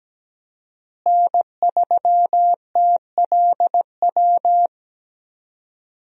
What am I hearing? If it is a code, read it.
Morse code, heard as a keyed tone, about 17 wpm, 700 Hz. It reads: N3TLW
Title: USA Call Signs - 5 Characters - Repeated 3x - 17wpm